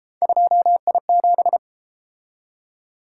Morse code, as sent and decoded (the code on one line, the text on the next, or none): ..--- .. --...
2I7